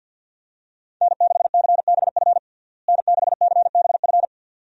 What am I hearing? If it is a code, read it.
Morse code, heard as a keyed tone, about 51 wpm, 700 Hz. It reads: N6CBF N6CBF
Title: USA Call Signs - 5 Characters - Repeated 2x - 50wpm